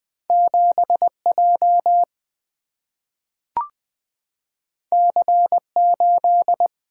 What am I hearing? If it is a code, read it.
Morse code, heard as a keyed tone, about 20 wpm, 700 Hz. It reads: 7 J C 8
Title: Single-Letter Single-Number 20wpm